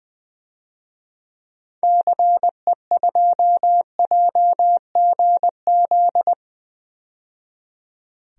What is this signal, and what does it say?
Morse code, heard as a keyed tone, about 20 wpm, 700 Hz. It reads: CE2JGZ
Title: DXCC-100 Call Signs - 6 Characters - 20wpm